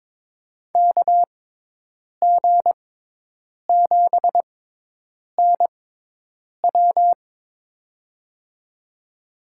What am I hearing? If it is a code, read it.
Morse code, heard as a keyed tone, about 22 wpm, spaced as for 8 wpm, 700 Hz. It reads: KG7NW
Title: USA Call Signs - 5 Characters - 6x Character Spacing - 22wpm